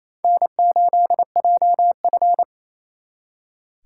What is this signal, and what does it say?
Morse code, heard as a keyed tone, about 35 wpm, 700 Hz. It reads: N8JF
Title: Speed Racing - USA Call Signs 28wpm